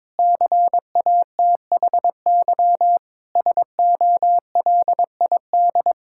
Morse code, heard as a keyed tone, about 22 wpm, 700 Hz. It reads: CATHY SOLID
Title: K1USN SST QSO Practice 22wpm